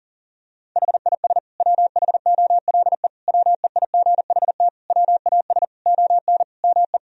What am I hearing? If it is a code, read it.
Morse code, heard as a keyed tone, about 40 wpm, 700 Hz. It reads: HIS WHOLE WEIGHT WAS ON ME
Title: Sentences from Top 1000 Words - Review Part 3 - 40wpm